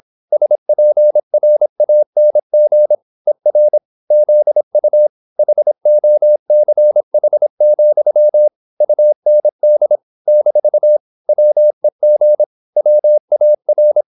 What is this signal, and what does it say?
Morse code, heard as a keyed tone, about 26 wpm, 600 Hz. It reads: SPRANG ER ZU HOCH, UND - WEG WAR